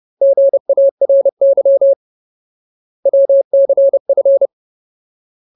Morse code, heard as a keyed tone, about 30 wpm, 550 Hz. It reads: GARY WCF
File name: Ultimate Word List / 30WPM Ultimate